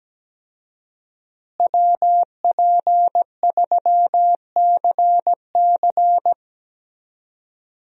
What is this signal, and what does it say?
Morse code, heard as a keyed tone, about 17 wpm, 700 Hz. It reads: WP3CC
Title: USA Call Signs - 5 Characters - Repeated 2x - 17wpm